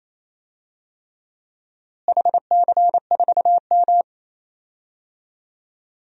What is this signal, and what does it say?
Morse code, heard as a keyed tone, about 27 wpm, 700 Hz. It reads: HC4M
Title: DXCC-100 Call Signs - 4 Characters - Repeated 3x - 28wpm